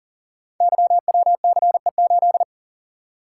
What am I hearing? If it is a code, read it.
Morse code, heard as a keyed tone, about 41 wpm, 700 Hz. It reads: YWCE8
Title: Code Groups - 5 Letter:Numbers - 40wpm